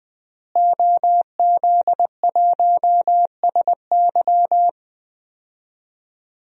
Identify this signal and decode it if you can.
Morse code, heard as a keyed tone, about 20 wpm, 700 Hz. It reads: OZ1SY
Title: DXCC-100 Call Signs - 5 Characters - Repeated 2x - 20wpm